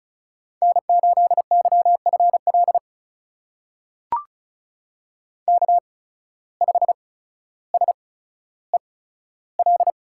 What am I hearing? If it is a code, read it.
Morse code, heard as a keyed tone, about 34 wpm, spaced as for 15 wpm, 700 Hz. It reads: N8YFL K5SEL
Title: USA Call Signs - 5 Characters - 8x Character Spacing - 35wpm